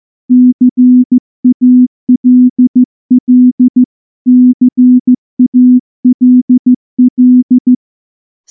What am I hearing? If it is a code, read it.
Morse code, heard as a keyed tone, about 14 wpm, 250 Hz. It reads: CALL CALL
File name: morse-training-15WPM